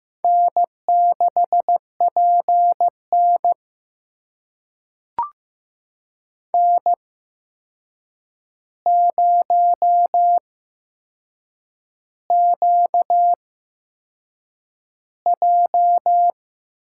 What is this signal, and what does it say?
Morse code, heard as a keyed tone, about 15 wpm, spaced as for 6 wpm, 700 Hz. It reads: N6PN N0QJ
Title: USA Call Signs - 4 Characters - 8x Character Spacing - 15wpm